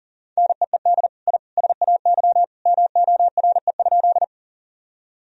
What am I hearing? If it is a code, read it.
Morse code, heard as a keyed tone, about 37 wpm, 700 Hz. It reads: NEED I SAY MORE?
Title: Sentences from Top 1000 Words - Review Part 1 - 40wpm